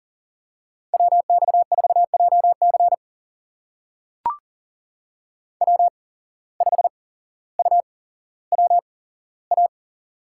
Morse code, heard as a keyed tone, about 40 wpm, spaced as for 17 wpm, 700 Hz. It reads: WX4JC W5UWA
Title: USA Call Signs - 5 Characters - 8x Character Spacing - 40wpm